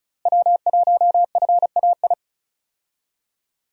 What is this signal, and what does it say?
Morse code, heard as a keyed tone, about 36 wpm, 700 Hz. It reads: W1FAI
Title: USA Call Signs - 5 Characters - Repeated 2x - 35wpm